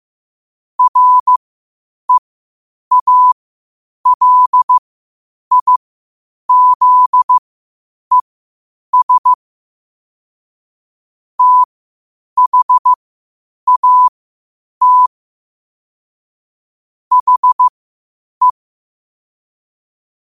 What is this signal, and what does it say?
Morse code, heard as a keyed tone, about 15 wpm, spaced as for 8 wpm, 1000 Hz. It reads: REALIZES THAT HE